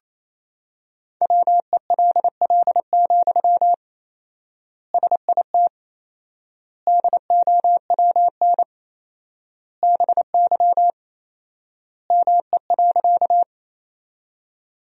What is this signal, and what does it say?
Morse code, heard as a keyed tone, about 28 wpm, 700 Hz. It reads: WELL, SIT DOWN BY ME.
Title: 4x Word-Spacing - Sentences from Top 500 Words - 28wpm